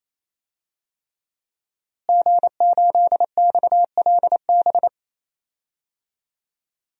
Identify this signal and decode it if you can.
Morse code, heard as a keyed tone, about 28 wpm, 700 Hz. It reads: G8XLB
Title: DXCC-100 Call Signs - 5 Characters - Repeated 3x - 28wpm